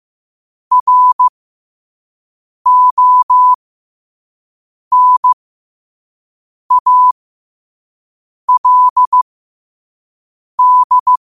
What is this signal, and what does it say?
Morse code, heard as a keyed tone, about 15 wpm, spaced as for 5 wpm, 1000 Hz. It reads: RONALD
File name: qotdfiles / 5wpm